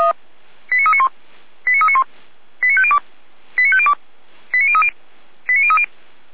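Five-tone selective call (ZVEI), heard as quick runs of five tones, tones of about 70 ms. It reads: 89391 89391 89502 89502 89030 89030
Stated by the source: ZVEI five-tone selective call received on an SDR.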